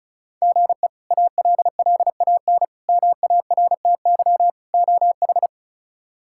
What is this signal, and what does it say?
Morse code, heard as a keyed tone, about 34 wpm, 700 Hz. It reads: GE ALLAN MARTY OH
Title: K1USN SST QSO Practice - 35wpm